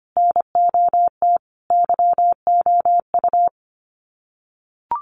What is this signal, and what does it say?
Morse code, heard as a keyed tone, about 25 wpm, 700 Hz. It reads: NOT YOU
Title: Sets of 3 Words 25wpm